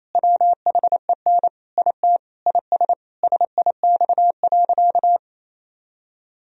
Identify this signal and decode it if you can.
Morse code, heard as a keyed tone, about 28 wpm, 700 Hz. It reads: WHEN IT IS SIX.
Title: Sentences from Top 1000 Words - Review Part 2 - 28wpm